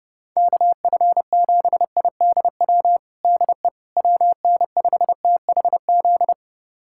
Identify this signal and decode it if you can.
Morse code, heard as a keyed tone, about 30 wpm, 700 Hz. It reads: KF7IDW DE WN5THZ
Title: USA Callsign DE Callsign - 6 Characters - 30wpm